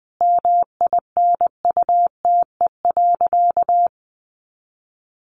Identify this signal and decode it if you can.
Morse code, heard as a keyed tone, about 20 wpm, 700 Hz. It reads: MINUTE.
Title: Sentences from Top 500 Words 20wpm